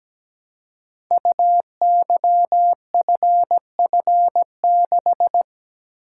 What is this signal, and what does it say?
Morse code, heard as a keyed tone, about 17 wpm, 700 Hz. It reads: UYFF6
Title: Code Groups - 5 Letter:Numbers - 17wpm